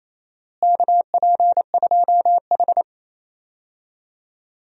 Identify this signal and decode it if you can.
Morse code, heard as a keyed tone, about 28 wpm, 700 Hz. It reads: KP2H
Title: DXCC-100 Call Signs - 4 Characters - Repeated 2x - 28wpm